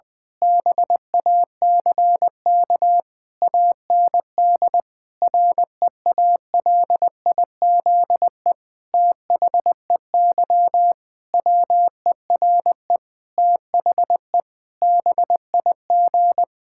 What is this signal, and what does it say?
Morse code, heard as a keyed tone, about 21 wpm, 700 Hz. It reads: BACK AND REALIZE THEY WERE THE BIG
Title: Difficult Sentences with Hints - 20wpm